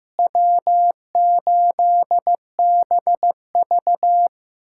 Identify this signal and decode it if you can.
Morse code, heard as a keyed tone, about 15 wpm, 700 Hz. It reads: W8BV
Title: USA Callsign DE Callsign - 4 Characters - 15wpm